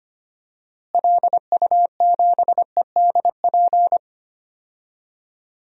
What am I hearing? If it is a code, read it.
Morse code, heard as a keyed tone, about 25 wpm, 700 Hz. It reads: LU7EDP
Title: DXCC-100 Call Signs - 6 Characters - Repeated 2x - 25wpm